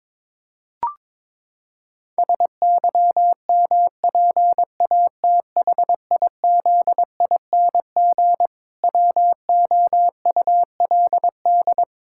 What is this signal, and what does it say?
Morse code, heard as a keyed tone, about 22 wpm, 700 Hz. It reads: SYMPATHIZING WOULD
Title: Pangram Challenge v2 - 22wpm